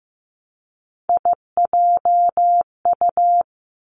Morse code, heard as a keyed tone, about 15 wpm, 700 Hz. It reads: IJU
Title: Three Challenging Letter-Number Combinations 15wpm